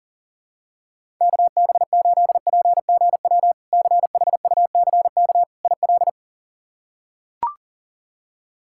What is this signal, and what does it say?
Morse code, heard as a keyed tone, about 40 wpm, 700 Hz. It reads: KB8PGW CHUCK IL
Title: K1USN SST QSO Practice - 40wpm